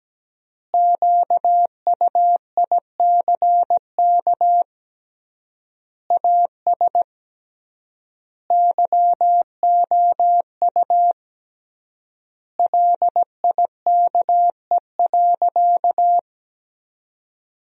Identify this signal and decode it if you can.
Morse code, heard as a keyed tone, about 17 wpm, 700 Hz. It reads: QUICK AS YOU LIKE.
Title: 3x Word-Spacing - Sentences from Top 500 Words - 17wpm